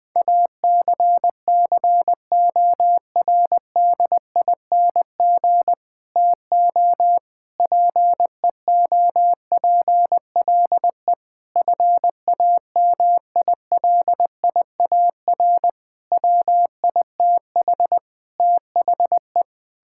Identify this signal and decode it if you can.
Morse code, heard as a keyed tone, about 20 wpm, 700 Hz. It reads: ACCORDING TO PEOPLE FAMILIAR WITH THE
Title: Sentences with 7-Word Phrases - Spoken Beginning - 20wpm